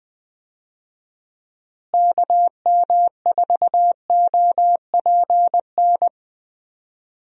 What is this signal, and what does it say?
Morse code, heard as a keyed tone, about 20 wpm, 700 Hz. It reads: KM4OPN
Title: USA Call Signs - 6 Characters - Repeated 2x - 20wpm